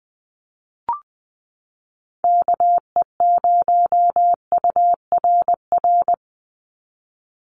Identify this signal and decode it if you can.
Morse code, heard as a keyed tone, about 20 wpm, 700 Hz. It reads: KE0URR
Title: USA Call Signs 20wpm